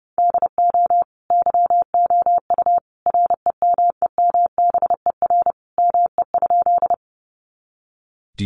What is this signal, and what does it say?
Morse code, heard as a keyed tone, about 30 wpm, 700 Hz. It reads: DO YOU REMEMBER ME?
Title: Sentences from Top 400 Words 30wpm